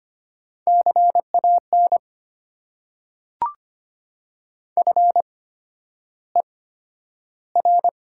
Morse code, heard as a keyed tone, about 25 wpm, spaced as for 11 wpm, 700 Hz. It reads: CAN FER
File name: Top 90 QSO Elements - 8x Character Spacing - 25wpm